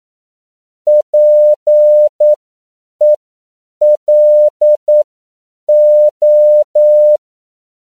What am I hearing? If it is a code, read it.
Morse code, heard as a keyed tone, about 9 wpm, spaced as for 7 wpm, 600 Hz. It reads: PELO